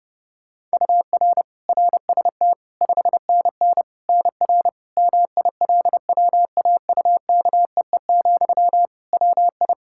Morse code, heard as a keyed tone, about 29 wpm, 700 Hz. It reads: UR RST 5NN NR MILWAUKEE, WI
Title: Practice QSO - Exchange 1 - Beginning - 30wpm